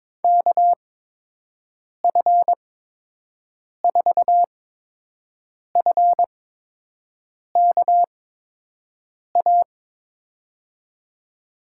Morse code, heard as a keyed tone, about 22 wpm, spaced as for 6 wpm, 700 Hz. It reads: KF4FKA
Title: USA Call Signs - 8x Character Spacing - 22wpm